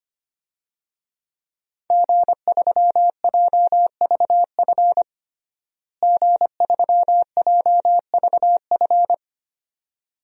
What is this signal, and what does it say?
Morse code, heard as a keyed tone, about 25 wpm, 700 Hz. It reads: G3JVF G3JVF
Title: DXCC-100 Call Signs - 5 Characters - Repeated 2x - 25wpm